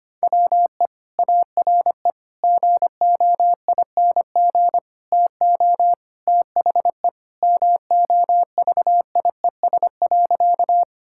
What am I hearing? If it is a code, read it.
Morse code, heard as a keyed tone, about 25 wpm, 700 Hz. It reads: WE ARE GOING TO THE MOVIES.
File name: practice / 25wpm / Sentences with 4-Word Phrases - 25wpm